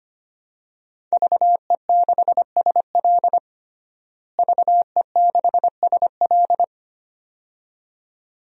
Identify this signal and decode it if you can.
Morse code, heard as a keyed tone, about 25 wpm, 700 Hz. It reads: VE6SL VE6SL
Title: DXCC-100 Call Signs - 5 Characters - Repeated 2x - 25wpm